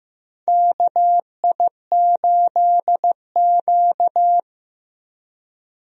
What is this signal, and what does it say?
Morse code, heard as a keyed tone, about 15 wpm, 700 Hz. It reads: KI8Q
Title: USA Call Signs - 4 Characters - Repeated 2x - 15wpm